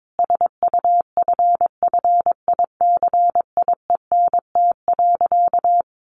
Morse code, heard as a keyed tone, about 22 wpm, 700 Hz. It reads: SUFFICIENT.
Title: Sentences from Top 1400 Words 22wpm